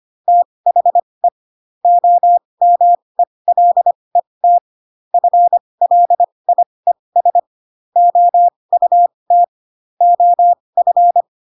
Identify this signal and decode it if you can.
Morse code, heard as a keyed tone, about 25 wpm, spaced as for 20 wpm, 700 Hz. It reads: THE OMELET FLIES OUT OF
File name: Pinocchio / Pinocchio ch5 15 25 ik1hge